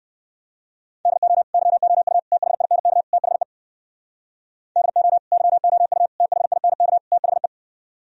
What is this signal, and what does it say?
Morse code, heard as a keyed tone, about 67 wpm, 700 Hz. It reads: DO YOU THINK THE DO YOU THINK THE